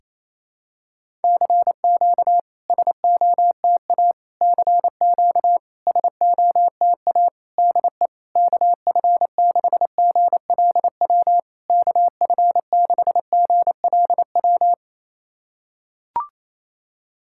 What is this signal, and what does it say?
Morse code, heard as a keyed tone, about 28 wpm, 700 Hz. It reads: CQ SOTA CQ SOTA DE KF6GLW KF6GLW
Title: SOTA QSO Practice 28wpm